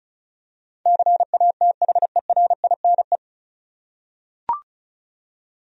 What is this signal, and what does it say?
Morse code, heard as a keyed tone, about 34 wpm, 700 Hz. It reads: CATHERINE
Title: USA Common Names - 35wpm